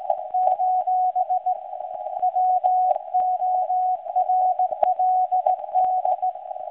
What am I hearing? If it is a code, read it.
Morse code, heard as a keyed tone, about 17 wpm, 740 Hz. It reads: : WORLD
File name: WebSDR records / SAQ recording start 17.2kHz